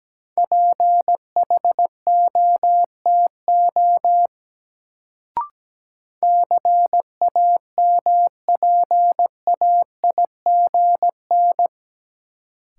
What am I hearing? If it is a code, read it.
Morse code, heard as a keyed tone, about 17 wpm, 700 Hz. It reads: PHOTO CAMPAIGN
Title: Sentences from Top 1200 Words - New Word Drill - 17wpm